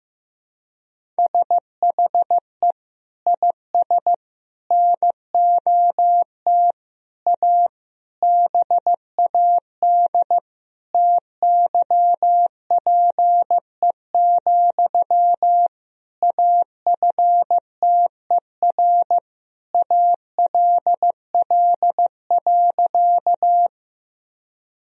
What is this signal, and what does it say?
Morse code, heard as a keyed tone, about 15 wpm, 700 Hz. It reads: SHE IS NOT A BAD TYPE, AFTER ALL.
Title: Sentences from Top 1000 Words - Review Part 3 - 15wpm